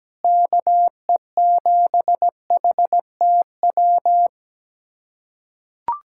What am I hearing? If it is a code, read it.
Morse code, heard as a keyed tone, about 17 wpm, 700 Hz. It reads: KE7HTW
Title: USA Callsign DE Callsign 17wpm